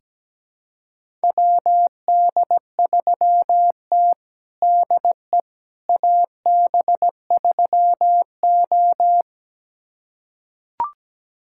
Morse code, heard as a keyed tone, about 17 wpm, 700 Hz. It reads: WD3T DE AB3O
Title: USA Callsign DE Callsign - 4 Characters - 17wpm